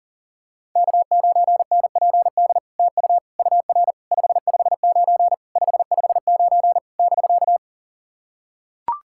Morse code, heard as a keyed tone, about 44 wpm, 700 Hz. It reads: K9NPD TU UR 559 559 <BK>
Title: SOTA QSO Practice 40wpm